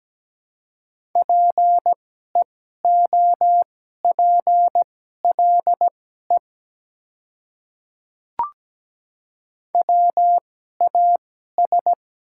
Morse code, heard as a keyed tone, about 17 wpm, spaced as for 12 wpm, 700 Hz. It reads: PEOPLE WAS
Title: 2x Character Spacing - Top 100 Words - 17wpm